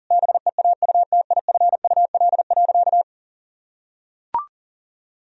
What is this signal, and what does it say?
Morse code, heard as a keyed tone, about 40 wpm, 700 Hz. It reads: BEAUTIFUL.
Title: Sentences from Top 1000 Words - Review Part 1 - 40wpm